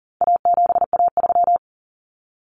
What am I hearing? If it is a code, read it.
Morse code, heard as a keyed tone, about 40 wpm, 700 Hz. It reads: A7A3
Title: Common Cut Numbers - 4-Digit 40wpm